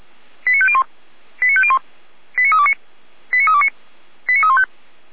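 Five-tone selective call (ZVEI), heard as quick runs of five tones, tones of about 70 ms. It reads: 89501 89501 89329 89329 89315
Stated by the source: SDR-received ZVEI five-tone selective call, address 89320.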